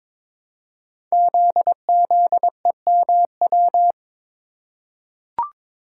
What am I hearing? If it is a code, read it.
Morse code, heard as a keyed tone, about 22 wpm, 700 Hz. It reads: ZZEMW
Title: Code Groups - 5 Letters - 22wpm